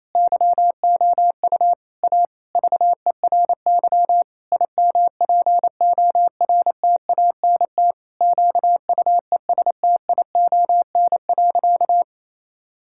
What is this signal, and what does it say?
Morse code, heard as a keyed tone, about 28 wpm, 700 Hz. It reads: YOU A VERY IMPORTANT QUESTION.
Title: Sentences from Top 1000 Words - Review Part 6 - 28wpm